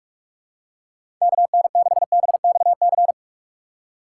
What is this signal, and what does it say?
Morse code, heard as a keyed tone, about 44 wpm, 700 Hz. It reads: KN6BXC